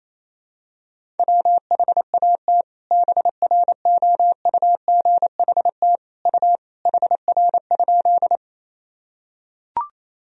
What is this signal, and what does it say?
Morse code, heard as a keyed tone, about 28 wpm, 700 Hz. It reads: WHAT BROUGHT U HR?
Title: Sentences from Top 500 Words and Abbreviations 28wpm